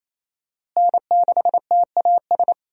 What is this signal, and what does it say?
Morse code, heard as a keyed tone, about 28 wpm, 700 Hz. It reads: N6TAS
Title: USA Call Signs - 5 Characters - Repeated 2x - 28wpm